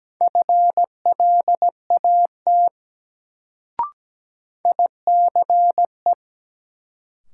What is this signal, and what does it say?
Morse code, heard as a keyed tone, about 17 wpm, 700 Hz. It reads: FLAT ICE
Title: Sentences from Top 1000 Words - New Word Drill - 17wpm